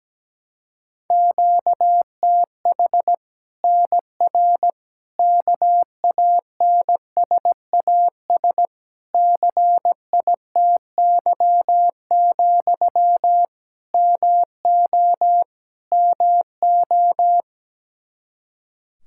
Morse code, heard as a keyed tone, about 17 wpm, 700 Hz. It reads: QTH NR KANSAS CITY, MO MO
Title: Practice QSO - Exchange 1 - Beginning - 17wpm